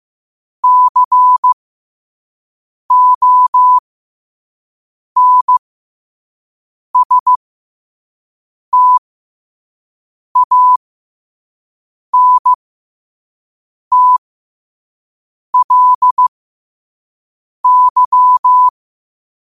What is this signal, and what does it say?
Morse code, heard as a keyed tone, about 15 wpm, spaced as for 5 wpm, 1000 Hz. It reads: CONSTANTLY
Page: New quotes every day in morse code at 5 Words per minute.